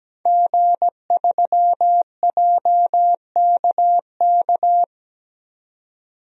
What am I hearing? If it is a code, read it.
Morse code, heard as a keyed tone, about 17 wpm, 700 Hz. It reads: G3JKK
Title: DXCC-100 Call Signs - 5 Characters - 17wpm